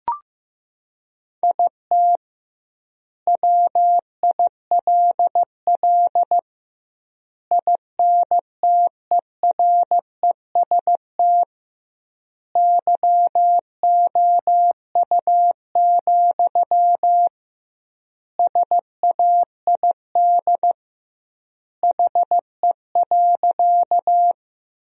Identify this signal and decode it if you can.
Morse code, heard as a keyed tone, about 15 wpm, 700 Hz. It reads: IT WILL INTEREST YOU, SAID HE.
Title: 2x Word-Spacing - Sentences from Top 400 Words - 15wpm